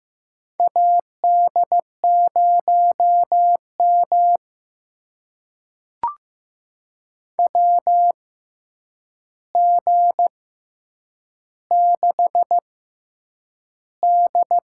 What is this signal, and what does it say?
Morse code, heard as a keyed tone, about 15 wpm, spaced as for 8 wpm, 700 Hz. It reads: AD0M WG6D
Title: USA Call Signs - 4 Characters - 6x Character Spacing - 15wpm